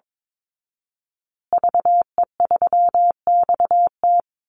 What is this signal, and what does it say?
Morse code, heard as a keyed tone, about 22 wpm, 700 Hz. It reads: VE3XT
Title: Call Signs Repeated 2x - 22wpm